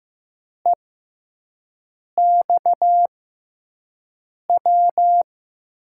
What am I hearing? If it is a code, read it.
Morse code, heard as a keyed tone, about 15 wpm, spaced as for 5 wpm, 700 Hz. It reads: EXW
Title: USA Call Signs - 6x Character Spacing - 15wpm